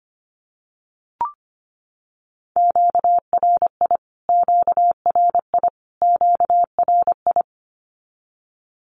Morse code, heard as a keyed tone, about 25 wpm, 700 Hz. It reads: QRI QRI QRI
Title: Q-Codes Repeated 3x 25wpm